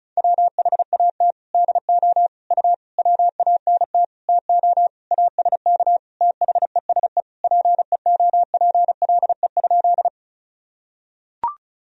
Morse code, heard as a keyed tone, about 36 wpm, 700 Hz. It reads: WHAT DO U WANT TO ASK THESE PEOPLE?
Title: Sentences from Top 200 Words and Abbreviations 35wpm